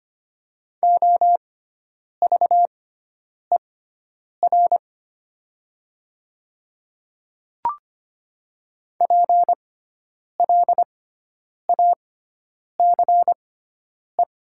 Morse code, heard as a keyed tone, about 25 wpm, spaced as for 9 wpm, 700 Hz. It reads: OVER PLACE
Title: 6x Character Spacing - Top 200 Words - 25wpm